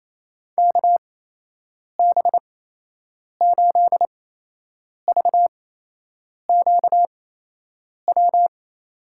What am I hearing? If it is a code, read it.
Morse code, heard as a keyed tone, about 28 wpm, spaced as for 8 wpm, 700 Hz. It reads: KB8VQW
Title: USA Call Signs - 8x Character Spacing - 28wpm